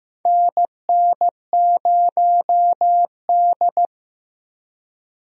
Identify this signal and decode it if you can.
Morse code, heard as a keyed tone, about 15 wpm, 700 Hz. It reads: NN0D
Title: USA Call Signs - 4 Characters - Repeated 2x - 15wpm